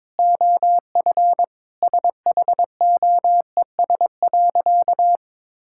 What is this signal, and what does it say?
Morse code, heard as a keyed tone, about 22 wpm, 700 Hz. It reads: OF SHOES.
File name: practice / 22wpm / Sentences with 3-Word Phrases - 22wpm